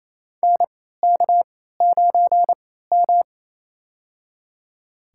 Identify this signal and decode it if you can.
Morse code, heard as a keyed tone, about 27 wpm, spaced as for 16 wpm, 700 Hz. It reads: NK9M
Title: USA Call Signs - 4 Characters - 3x Character Spacing - 28wpm